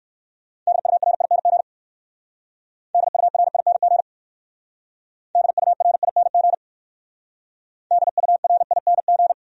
Morse code, heard as a keyed tone, about 59 wpm, 700 Hz. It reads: DURING DURING DURING DURING